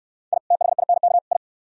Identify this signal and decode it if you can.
Morse code, heard as a keyed tone, about 67 wpm, 700 Hz. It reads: I THINK I